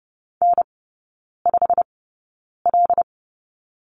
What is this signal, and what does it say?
Morse code, heard as a keyed tone, about 30 wpm, spaced as for 9 wpm, 700 Hz. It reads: N5L
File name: course / 30:10wpm / 087 Lesson 15e - Number 5 - Three Characters 30wpm